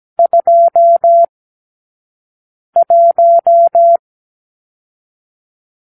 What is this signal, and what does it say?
Morse code, heard as a keyed tone, about 17 wpm, 680 Hz. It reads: 2 1
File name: k7qo / Lesson 44 (1-3)